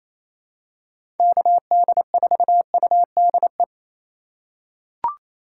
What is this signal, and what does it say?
Morse code, heard as a keyed tone, about 27 wpm, 700 Hz. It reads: KD4UDE
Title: USA Call Signs - 6 Characters - 4x Character Spacing - 28wpm